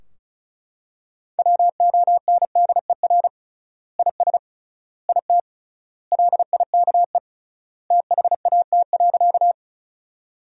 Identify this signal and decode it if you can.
Morse code, heard as a keyed tone, about 36 wpm, 700 Hz. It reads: WONDER IS IT LIKE THAT.
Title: 3x Word-Spacing - Sentences from Top 500 Words - 35wpm